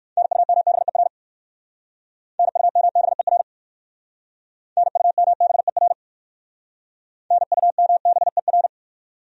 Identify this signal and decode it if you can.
Morse code, heard as a keyed tone, about 58 wpm, 700 Hz. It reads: NUMBER NUMBER NUMBER NUMBER